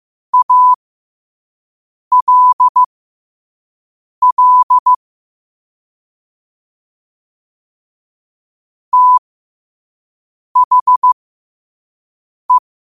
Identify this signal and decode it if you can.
Morse code, heard as a keyed tone, about 15 wpm, spaced as for 5 wpm, 1000 Hz. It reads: ALL THE
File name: qotdfiles / 5wpm